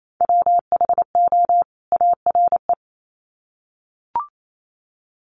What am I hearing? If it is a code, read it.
Morse code, heard as a keyed tone, about 28 wpm, 700 Hz. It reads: WHO ARE
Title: Sets of 2 Words 28wpm